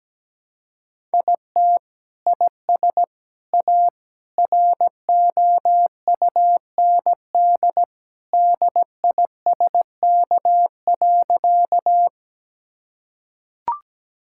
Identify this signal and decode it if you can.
Morse code, heard as a keyed tone, about 17 wpm, 700 Hz. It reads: IT IS A ROUND DISK.
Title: Sentences from Top 1000 Words - Review Part 6 - 17wpm